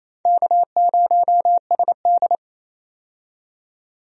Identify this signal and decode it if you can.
Morse code, heard as a keyed tone, about 28 wpm, 690 Hz. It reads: K0SD
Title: USA Call Signs - 4 Characters - 28wpm